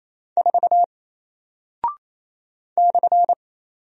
Morse code, heard as a keyed tone, about 28 wpm, 700 Hz. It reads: 4 /
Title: Single Character (Letter:Number:Punctuation:Prosign) Alt 28wpm